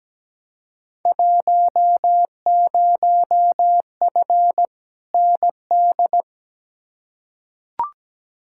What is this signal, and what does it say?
Morse code, heard as a keyed tone, about 17 wpm, 700 Hz. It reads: 10F ND
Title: Field Day Exchanges Encore - 17wpm